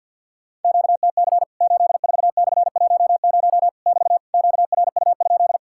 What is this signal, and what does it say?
Morse code, heard as a keyed tone, about 50 wpm, 700 Hz. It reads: QTC 84/10 <BT> QRU?
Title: Worked All Europe DX Contest - QTC Exchange with Repeats - 50wpm